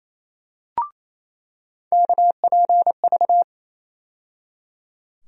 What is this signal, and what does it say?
Morse code, heard as a keyed tone, about 28 wpm, 700 Hz. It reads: KPV
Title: USA Call Sign Postfixes - 3 Letters 28wpm